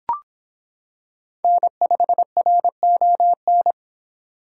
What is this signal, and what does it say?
Morse code, heard as a keyed tone, about 26 wpm, 700 Hz. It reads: N5RON